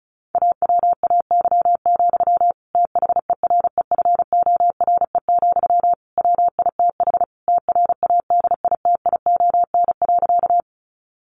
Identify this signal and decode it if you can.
Morse code, heard as a keyed tone, about 34 wpm, 700 Hz. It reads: AWAY, THEREFORE, WITH TRADITION.
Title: Sentences from Top 1500 Words 35wpm